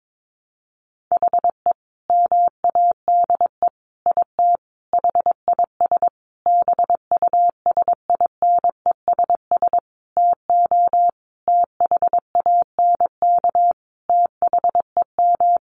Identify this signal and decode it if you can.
Morse code, heard as a keyed tone, about 22 wpm, 700 Hz. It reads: HE MADE IT HIS BUSINESS TO THANK THEM
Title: Sentences from Top 1400 Words 22wpm